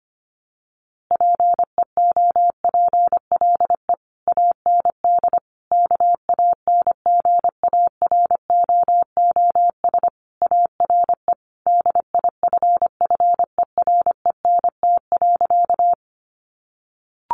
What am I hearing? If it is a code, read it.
Morse code, heard as a keyed tone, about 25 wpm, 700 Hz. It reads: PEOPLE AND KANGAROOS ARE DIFFERENT.
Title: Easy Phrases 25wpm